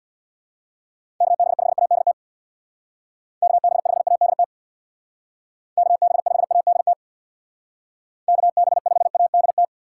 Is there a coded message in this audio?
Yes: Morse code from 1.2 to 9.6 s, heard as a keyed tone, about 65 wpm, 700 Hz. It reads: KB5ADT KB5ADT KB5ADT KB5ADT